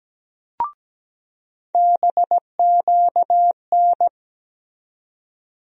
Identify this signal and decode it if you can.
Morse code, heard as a keyed tone, about 17 wpm, 700 Hz. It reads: BQN
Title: USA Call Sign Postfixes - 3 Letters 17wpm